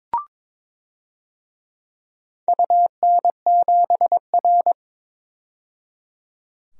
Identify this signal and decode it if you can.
Morse code, heard as a keyed tone, about 21 wpm, 700 Hz. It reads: UN7R
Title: DXCC-100 Call Signs - 4 Characters - 22wpm